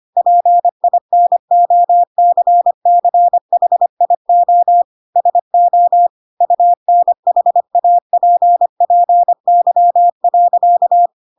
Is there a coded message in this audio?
Yes: Morse code from 0.2 to 11.1 s, heard as a keyed tone, about 25 wpm, 700 Hz. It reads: PINOCCHIO SO UNHAPPY.